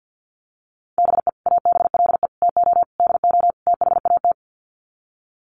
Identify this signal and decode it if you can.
Morse code, heard as a keyed tone, about 50 wpm, 700 Hz. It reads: BE ABLE TO DO THAT
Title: Sets of 5 Words Encore2 50wpm